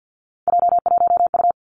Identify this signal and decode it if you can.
Morse code, heard as a keyed tone, about 50 wpm, 700 Hz. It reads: J1U
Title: Three Challenging Letter-Number Combinations 50wpm